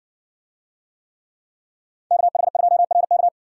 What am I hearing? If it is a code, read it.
Morse code, heard as a keyed tone, about 60 wpm, 700 Hz. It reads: KH2RK